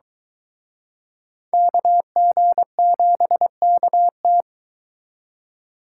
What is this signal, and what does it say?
Morse code, heard as a keyed tone, about 23 wpm, 700 Hz. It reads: KG7KT